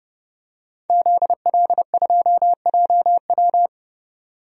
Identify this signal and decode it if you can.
Morse code, heard as a keyed tone, about 29 wpm, 700 Hz. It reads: ZL2JW
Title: DXCC-100 Call Signs - 5 Characters - 30wpm